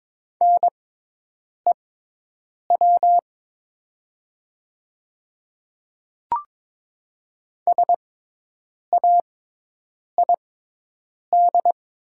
Morse code, heard as a keyed tone, about 22 wpm, spaced as for 8 wpm, 700 Hz. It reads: NEW SAID
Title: 6x Character Spacing - Top 200 Words - 22wpm